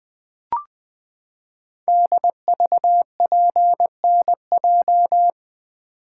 Morse code, heard as a keyed tone, about 20 wpm, 700 Hz. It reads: DVPNJ
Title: Code Groups - 5 Letters - 20wpm